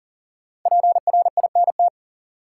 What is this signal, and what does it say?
Morse code, heard as a keyed tone, about 40 wpm, 700 Hz. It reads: PRINT
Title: Sentences from Top 1400 Words - New Word Drill Alt - 40wpm